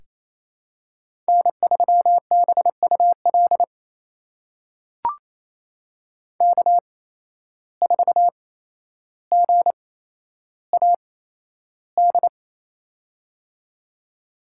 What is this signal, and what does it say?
Morse code, heard as a keyed tone, about 27 wpm, spaced as for 12 wpm, 700 Hz. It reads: N3BUL K4GAD
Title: USA Call Signs - 5 Characters - 8x Character Spacing - 28wpm